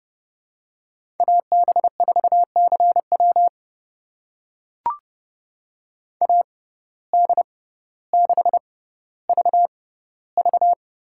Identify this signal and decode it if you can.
Morse code, heard as a keyed tone, about 30 wpm, spaced as for 15 wpm, 700 Hz. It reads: AB4CW AD6VV
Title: USA Call Signs - 5 Characters - 6x Character Spacing - 30wpm